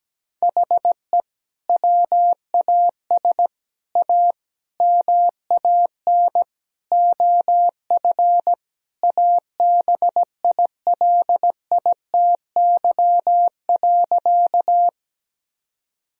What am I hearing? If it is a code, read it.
Morse code, heard as a keyed tone, about 17 wpm, 700 Hz. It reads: HE WAS A MAN OF ABILITY.
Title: Sentences from Top 1000 Words - Review Part 3 - 17wpm